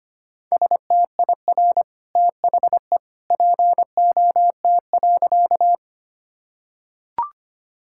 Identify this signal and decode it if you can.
Morse code, heard as a keyed tone, about 25 wpm, 700 Hz. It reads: STIR THE POT.
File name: Easy Sentences with Hints - 25wpm